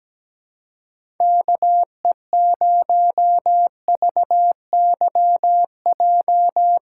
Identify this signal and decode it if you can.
Morse code, heard as a keyed tone, about 17 wpm, 700 Hz. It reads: KE0VYJ
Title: USA Call Signs - 4x Character Spacing - 17wpm